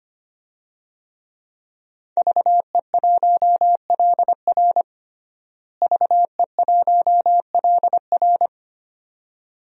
Morse code, heard as a keyed tone, about 25 wpm, 700 Hz. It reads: VE1LR VE1LR
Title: DXCC-100 Call Signs - 5 Characters - Repeated 2x - 25wpm